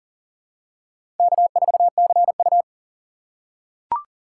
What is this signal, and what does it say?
Morse code, heard as a keyed tone, about 37 wpm, 700 Hz. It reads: K4CU
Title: USA Call Signs - 4 Characters - 4x Character Spacing - 40wpm